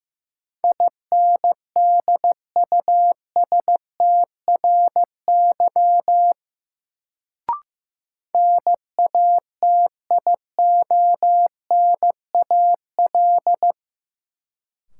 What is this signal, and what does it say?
Morse code, heard as a keyed tone, about 15 wpm, 700 Hz. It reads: INDUSTRY NATIONAL
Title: Sentences from Top 1000 Words - Review Part 4 - New Word Drill - 15wpm